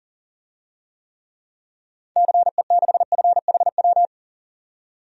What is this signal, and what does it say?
Morse code, heard as a keyed tone, about 40 wpm, 700 Hz. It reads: CE6FHW
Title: DXCC-100 Call Signs - 6 Characters - 40wpm